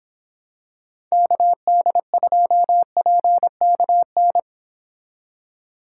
Morse code, heard as a keyed tone, about 26 wpm, 700 Hz. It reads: KD2PKN